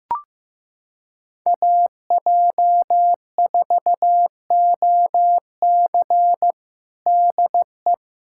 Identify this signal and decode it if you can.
Morse code, heard as a keyed tone, about 14 wpm, 700 Hz. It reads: AJ4OC DE
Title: K1USN SST QSO Practice 15wpm